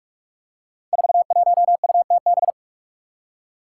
Morse code, heard as a keyed tone, about 43 wpm, 700 Hz. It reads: 41UTB
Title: Five Challenging Letter-Number Combinations 45wpm